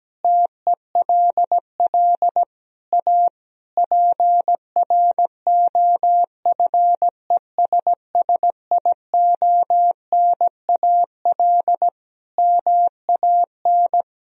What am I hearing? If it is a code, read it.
Morse code, heard as a keyed tone, about 17 wpm, 700 Hz. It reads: TELL A PROFESSIONAL MAN
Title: Sentences from Top 1000 Words - Review Part 4 - 17wpm